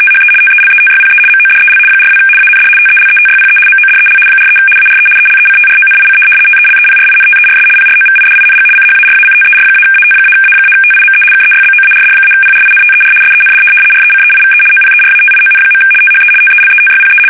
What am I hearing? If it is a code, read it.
RTTY, about 75 baud. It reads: THE TELETYPE MODEL 42 AND TELETYPE MODEL 43 ARE ELECTRONIC TELEPRINTERS PROVIDING CHARACTER-AT-A-TIME KEYBOARD-PRINTER SEND RECEIVE OPERATION. THE TELETYPE
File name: fsk850-75B / teletype-corp